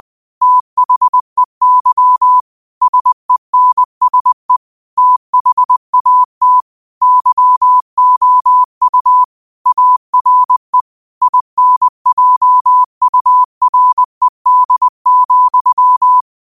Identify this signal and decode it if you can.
Morse code, heard as a keyed tone, about 20 wpm, 1000 Hz. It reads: THEY SENSE THAT YOU ARE INJURED,
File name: qotdfiles / 20wpm